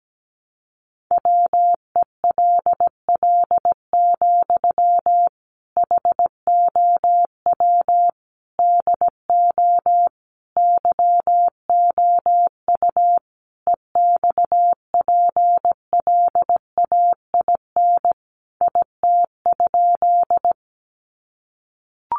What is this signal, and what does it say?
Morse code, heard as a keyed tone, about 17 wpm, 700 Hz. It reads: WELL, HOW DO YOU EXPLAIN IT?
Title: Sentences from Top 1100 Words 17wpm